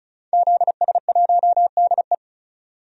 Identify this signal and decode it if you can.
Morse code, heard as a keyed tone, about 34 wpm, 700 Hz. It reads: ZS1DE
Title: DXCC-100 Call Signs - 5 Characters - 35wpm